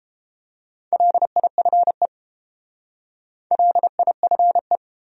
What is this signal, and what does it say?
Morse code, heard as a keyed tone, about 30 wpm, 700 Hz. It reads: LIFE LIFE